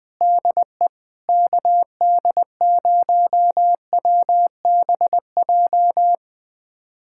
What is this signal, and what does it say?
Morse code, heard as a keyed tone, about 20 wpm, 700 Hz. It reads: DE KD0WBJ
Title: USA Callsign DE Callsign - 6 Characters - 20wpm